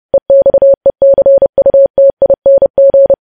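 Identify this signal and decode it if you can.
Morse code, heard as a keyed tone, about 30 wpm, 550 Hz. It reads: EXECUTING
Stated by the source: Morse practice of news headlines at 30wpm.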